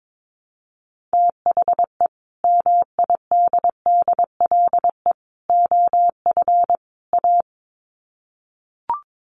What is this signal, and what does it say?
Morse code, heard as a keyed tone, about 22 wpm, 700 Hz. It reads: THE MIDDLE OF A
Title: Sets of 4 Words 22wpm